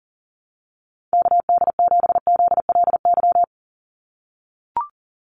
Morse code, heard as a keyed tone, about 40 wpm, 700 Hz. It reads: KD7ZLY
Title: USA Call Signs 40wpm